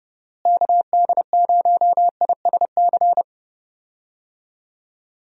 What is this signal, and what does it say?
Morse code, heard as a keyed tone, about 32 wpm, 700 Hz. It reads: KD0ISC
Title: USA Call Signs - 6 Characters - Repeated 3x - 30wpm